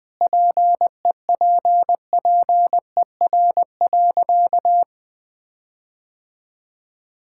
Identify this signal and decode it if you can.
Morse code, heard as a keyed tone, about 20 wpm, 700 Hz. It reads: PEPPER.
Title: Sentences with 5-Word Phrases - 20wpm